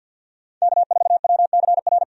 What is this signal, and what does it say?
Morse code, heard as a keyed tone, about 50 wpm, 700 Hz. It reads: K4WCR
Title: USA Call Signs - 5 Characters - 50wpm